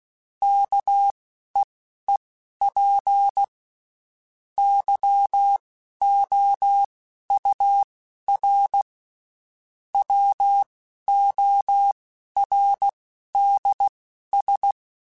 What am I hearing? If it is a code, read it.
Morse code, heard as a keyed tone, about 16 wpm, spaced as for 11 wpm, 800 Hz. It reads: KEEP YOUR WORDS
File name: Spaced / Chuckles Spaced-mp3s / SoftWords sp @15